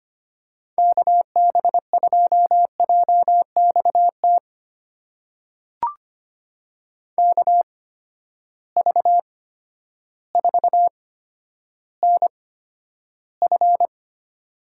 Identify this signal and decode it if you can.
Morse code, heard as a keyed tone, about 25 wpm, spaced as for 11 wpm, 700 Hz. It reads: KB2JXT KV4NF
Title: USA Call Signs - 8x Character Spacing - 25wpm